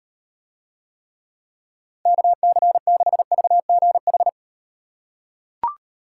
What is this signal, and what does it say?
Morse code, heard as a keyed tone, about 36 wpm, 700 Hz. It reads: KC6VGH